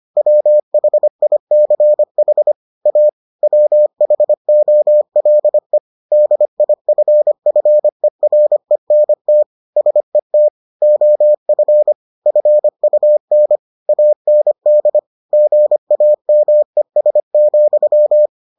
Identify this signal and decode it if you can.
Morse code, heard as a keyed tone, about 25 wpm, 600 Hz. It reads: WHICH A WHOLE DIFFERENT SET OF FUN AND GAMES,